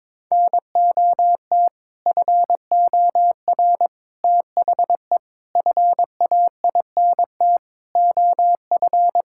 Morse code, heard as a keyed tone, about 22 wpm, 700 Hz. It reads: NOT FOR THE FAINT OF
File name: practice / 22wpm / Sets of 6 Words - 22wpm